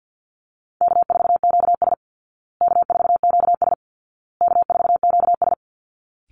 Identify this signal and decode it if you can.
Morse code, heard as a keyed tone, about 50 wpm, 700 Hz. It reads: K4QS K4QS K4QS
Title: Call Signs Repeated 2x - 50wpm